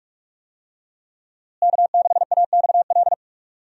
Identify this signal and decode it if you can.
Morse code, heard as a keyed tone, about 46 wpm, 700 Hz. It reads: K6AXL
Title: USA Call Signs - 5 Characters - Repeated 3x - 45wpm